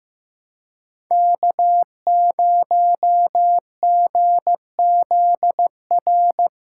Morse code, heard as a keyed tone, about 15 wpm, 700 Hz. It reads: K0GZR
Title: USA Call Signs - 5 Characters - 2x Character Spacing - 15wpm